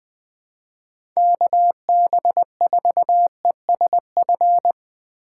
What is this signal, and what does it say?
Morse code, heard as a keyed tone, about 21 wpm, 700 Hz. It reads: KB4ESF